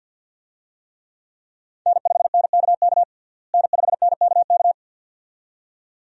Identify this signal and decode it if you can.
Morse code, heard as a keyed tone, about 52 wpm, 700 Hz. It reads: N5NKK N5NKK
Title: USA Call Signs - 5 Characters - Repeated 2x - 50wpm